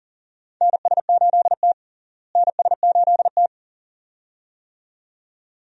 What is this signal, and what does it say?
Morse code, heard as a keyed tone, about 40 wpm, 700 Hz. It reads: NS8T NS8T
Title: USA Call Signs - 4 Characters - Repeated 2x - 40wpm